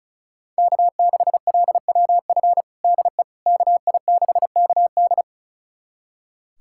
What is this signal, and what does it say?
Morse code, heard as a keyed tone, about 36 wpm, 700 Hz. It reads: K6LWF DE KI6KD
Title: USA Callsign DE Callsign - 5 Characters - 35wpm